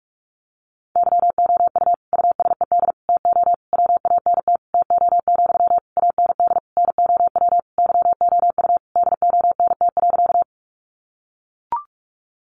Morse code, heard as a keyed tone, about 44 wpm, 700 Hz. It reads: YOU USED TO WANT TO, AND NOW YOU DONT.
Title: Sentences from Top 600 Words 45wpm